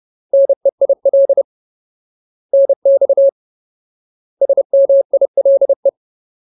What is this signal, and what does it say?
Morse code, heard as a keyed tone, about 30 wpm, 550 Hz. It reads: NEIL NX SMILE
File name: Ultimate Word List / 30WPM Ultimate